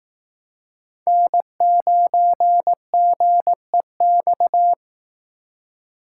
Morse code, heard as a keyed tone, about 18 wpm, 700 Hz. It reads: N9GEX
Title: Speed Racing - USA 1x3 Call Signs 15wpm